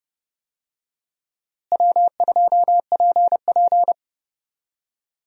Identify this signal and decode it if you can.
Morse code, heard as a keyed tone, about 30 wpm, 700 Hz. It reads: W2PP
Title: USA Call Signs - 4 Characters - Repeated 3x - 30wpm